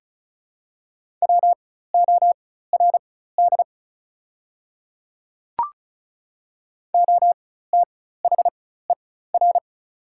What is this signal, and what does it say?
Morse code, heard as a keyed tone, about 36 wpm, spaced as for 16 wpm, 700 Hz. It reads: WORD OTHER
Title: 4x Character Spacing - Top 100 Words - 35wpm